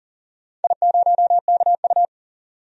Morse code, heard as a keyed tone, about 39 wpm, 700 Hz. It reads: I0KU
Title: DXCC-100 Call Signs - 4 Characters - 40wpm